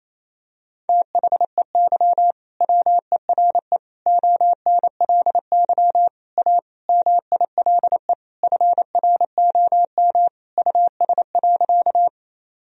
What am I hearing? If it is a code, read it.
Morse code, heard as a keyed tone, about 27 wpm, 700 Hz. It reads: THEY WERE ONLY A MILE FROM US.
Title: Sentences from Top 1000 Words - Review Part 1 - 28wpm